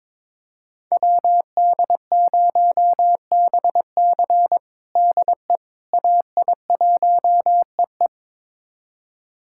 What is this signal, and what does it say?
Morse code, heard as a keyed tone, about 22 wpm, 700 Hz. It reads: WD0BC DE AI1EE
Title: USA Callsign DE Callsign - 5 Characters - 22wpm